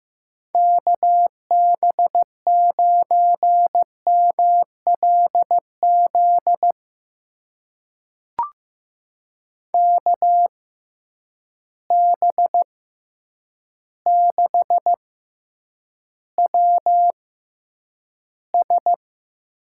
Morse code, heard as a keyed tone, about 15 wpm, spaced as for 8 wpm, 700 Hz. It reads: KB9MLZ KB6WS
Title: USA Call Signs - 6 Characters - 6x Character Spacing - 15wpm